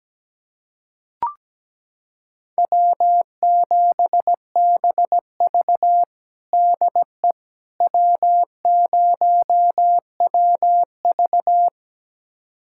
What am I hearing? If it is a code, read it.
Morse code, heard as a keyed tone, about 17 wpm, 700 Hz. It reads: W7BV DE W0WV
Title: USA Callsign DE Callsign - 4 Characters - 17wpm